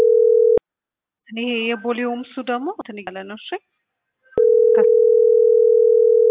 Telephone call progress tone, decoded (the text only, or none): ringback tone